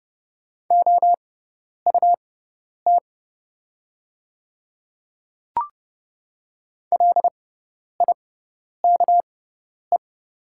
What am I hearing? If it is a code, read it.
Morse code, heard as a keyed tone, about 30 wpm, spaced as for 10 wpm, 700 Hz. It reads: OUT LIKE
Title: 6x Character Spacing - Top 300 Words - 30wpm